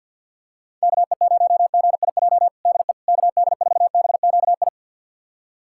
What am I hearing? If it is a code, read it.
Morse code, heard as a keyed tone, about 52 wpm, 700 Hz. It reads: KE0GIJ DE KD4BQI
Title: USA Callsign DE Callsign - 6 Characters - 50wpm